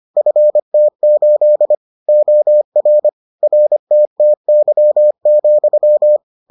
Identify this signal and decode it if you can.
Morse code, heard as a keyed tone, about 25 wpm, 600 Hz. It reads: FT8 OR RTTY,